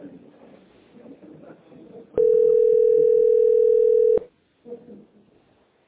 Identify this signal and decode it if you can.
Ringback tone (North America), 440 + 480 Hz.